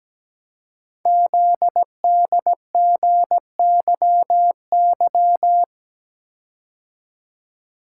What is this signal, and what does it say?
Morse code, heard as a keyed tone, about 17 wpm, 700 Hz. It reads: ZDGYY
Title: Code Groups - 5 Letters - 17wpm